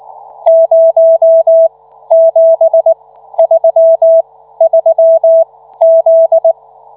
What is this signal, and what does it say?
Morse code, heard as a keyed tone, about 19 wpm, spaced as for 13 wpm, 660 Hz. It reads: 0733Z